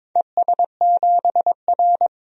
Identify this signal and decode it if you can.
Morse code, heard as a keyed tone, about 21 wpm, 700 Hz. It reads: ES7R
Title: DXCC-100 Call Signs - 4 Characters - 22wpm